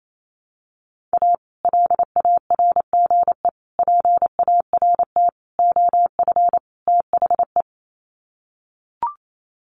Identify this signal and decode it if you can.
Morse code, heard as a keyed tone, about 28 wpm, 700 Hz. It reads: A LARGE PART OF THE
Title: Sets of 5 Words Encore2 28wpm